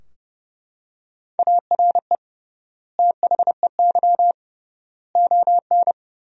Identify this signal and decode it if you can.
Morse code, heard as a keyed tone, about 32 wpm, 700 Hz. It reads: ARE THEY ON
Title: 3x Word-Spacing - Sentences from Top 500 Words - 30wpm